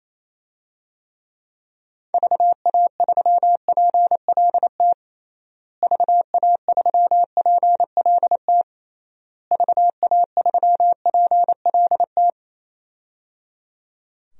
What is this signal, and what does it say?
Morse code, heard as a keyed tone, about 28 wpm, 700 Hz. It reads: VA3PLT VA3PLT VA3PLT
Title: DXCC-100 Call Signs - 6 Characters - Repeated 3x - 28wpm